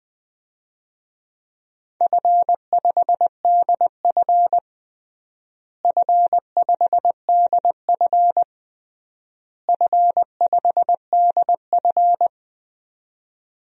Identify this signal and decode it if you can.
Morse code, heard as a keyed tone, about 20 wpm, 700 Hz. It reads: F5DF F5DF F5DF
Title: DXCC-100 Call Signs - 4 Characters - Repeated 3x - 20wpm